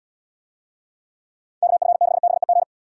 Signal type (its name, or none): Morse code